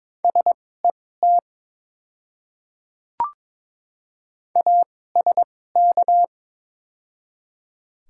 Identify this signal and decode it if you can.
Morse code, heard as a keyed tone, about 21 wpm, spaced as for 16 wpm, 700 Hz. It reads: SET ASK
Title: 2x Character Spacing - Top 200 Words - 22wpm